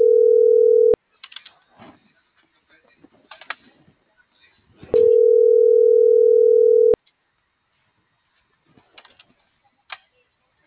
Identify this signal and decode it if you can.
Ringback tone (North America), 440 + 480 Hz.